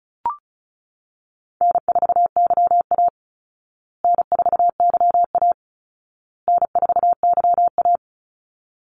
Morse code, heard as a keyed tone, about 34 wpm, 700 Hz. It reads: N4YA N4YA N4YA
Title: Easy Call Signs Repeated 3x - 35wpm